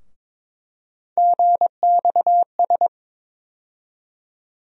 Morse code, heard as a keyed tone, about 22 wpm, 700 Hz. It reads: GXS
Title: USA Call Sign Postfixes - 3 Letters 22wpm